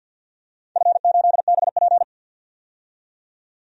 Morse code, heard as a keyed tone, about 52 wpm, 700 Hz. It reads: F8BP
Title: DXCC-100 Call Signs - 4 Characters - 50wpm